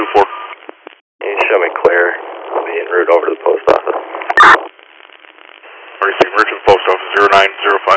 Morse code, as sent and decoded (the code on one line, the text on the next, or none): none